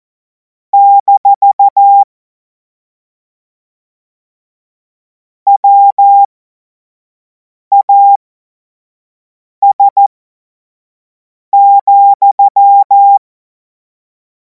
Morse code, heard as a keyed tone, about 14 wpm, spaced as for 5 wpm, 800 Hz. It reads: - WAS,
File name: Morsekurs / Witze / Witz 17. Februar 2026 25 F70